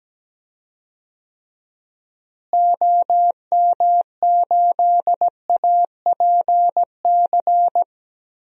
Morse code, heard as a keyed tone, about 17 wpm, 700 Hz. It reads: OM8APC
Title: DXCC-100 Call Signs - 6 Characters - Repeated 3x - 17wpm